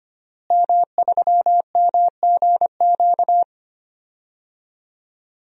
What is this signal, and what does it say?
Morse code, heard as a keyed tone, about 25 wpm, 700 Hz. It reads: M3MGQ
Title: DXCC-100 Call Signs - 5 Characters - Repeated 3x - 25wpm